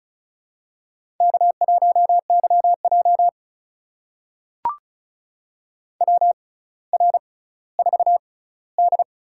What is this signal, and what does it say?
Morse code, heard as a keyed tone, about 36 wpm, spaced as for 18 wpm, 700 Hz. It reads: K1YJ WR4D
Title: USA Call Signs - 4 Characters - 6x Character Spacing - 35wpm